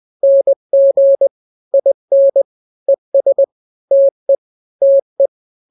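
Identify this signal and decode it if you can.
Morse code, heard as a keyed tone, about 20 wpm, 550 Hz. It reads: NG IN ES TE TE
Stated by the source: Morse drill of two-letter n-grams at 20/18 WPM (Farnsworth), from KK7RR.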